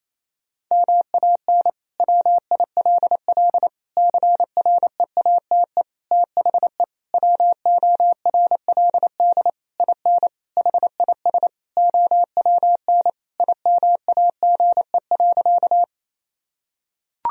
Morse code, heard as a keyed tone, about 28 wpm, 700 Hz. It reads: MAN WILL CREATE THE WORLD IN HIS OWN IMAGE.
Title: Sentences from Top 1000 Words - Review Part 4 - 28wpm